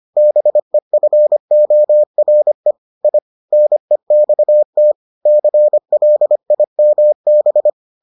Morse code, heard as a keyed tone, about 25 wpm, 600 Hz. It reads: BEFORE I NEXT CLIMB